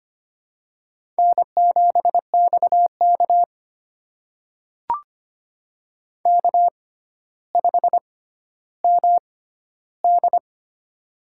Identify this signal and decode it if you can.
Morse code, heard as a keyed tone, about 25 wpm, spaced as for 13 wpm, 700 Hz. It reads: N7XK K5MD
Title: USA Call Signs - 4 Characters - 6x Character Spacing - 25wpm